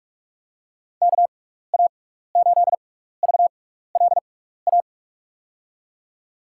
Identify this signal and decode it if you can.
Morse code, heard as a keyed tone, about 46 wpm, spaced as for 15 wpm, 700 Hz. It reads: KA8VLA
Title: USA Call Signs - 6x Character Spacing - 45wpm